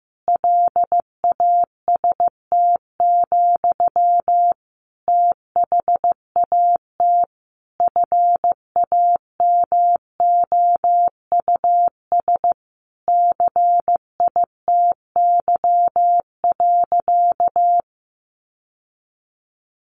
Morse code, heard as a keyed tone, about 15 wpm, 700 Hz. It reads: LAST, THAT FAMOUS CITY.
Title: Sentences from Top 1500 Words 15wpm